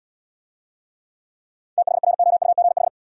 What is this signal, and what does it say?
Morse code, heard as a keyed tone, about 67 wpm, 700 Hz. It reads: THROUGH